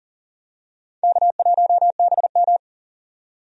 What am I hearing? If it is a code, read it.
Morse code, heard as a keyed tone, about 40 wpm, 700 Hz. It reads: K1BM
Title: USA Call Signs - 4 Characters - 40wpm